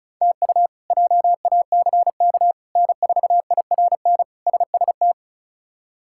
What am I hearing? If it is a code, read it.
Morse code, heard as a keyed tone, about 34 wpm, 700 Hz. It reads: TU JACK N4IRN SST
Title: K1USN SST QSO Practice - 35wpm